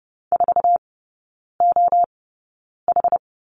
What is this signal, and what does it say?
Morse code, heard as a keyed tone, about 30 wpm, spaced as for 9 wpm, 700 Hz. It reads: 4OH
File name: Lesson 18e - Letter m - Three Characters 30wpm